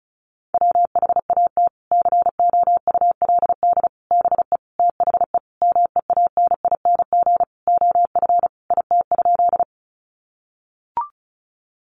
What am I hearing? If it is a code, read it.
Morse code, heard as a keyed tone, about 34 wpm, 700 Hz. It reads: WHAT COULD BE THE MEANING OF IT?
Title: Sentences from Top 900 Words 35wpm